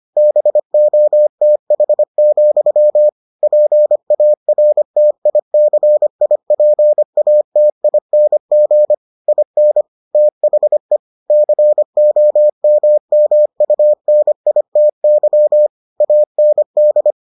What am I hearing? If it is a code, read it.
Morse code, heard as a keyed tone, about 25 wpm, 600 Hz. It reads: BOTH, PARTICIPATING IN THE COMMUNITY AND